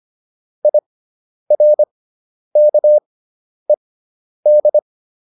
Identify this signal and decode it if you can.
Morse code, heard as a keyed tone, about 25 wpm, spaced as for 10 wpm, 600 Hz. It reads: IRKED